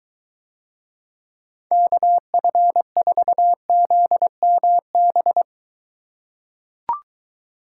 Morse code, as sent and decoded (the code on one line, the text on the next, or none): -.- ..-. ....- --.. -- -...
KF4ZMB